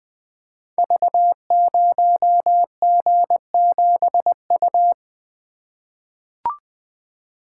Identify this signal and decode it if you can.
Morse code, heard as a keyed tone, about 20 wpm, 700 Hz. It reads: V0G7U
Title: Code Groups - 5 Letter:Numbers - 20wpm